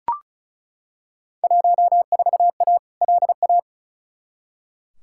Morse code, heard as a keyed tone, about 36 wpm, 690 Hz. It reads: 14A LA
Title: Field Day Exchanges Encore - 35wpm